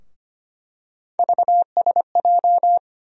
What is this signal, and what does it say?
Morse code, heard as a keyed tone, about 25 wpm, 700 Hz. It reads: VSJ
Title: USA Call Sign Postfixes - 3 Letters 25wpm